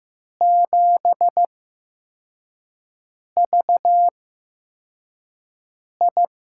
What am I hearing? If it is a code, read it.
Morse code, heard as a keyed tone, about 16 wpm, spaced as for 4 wpm, 700 Hz. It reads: 7VI
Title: USA Call Signs - 6 Characters - 8x Character Spacing - 15wpm